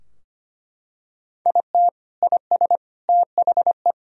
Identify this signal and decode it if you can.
Morse code, heard as a keyed tone, about 25 wpm, 700 Hz. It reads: IT IS THE
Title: Sentences from Top 1000 Words - Review Part 3 - 25wpm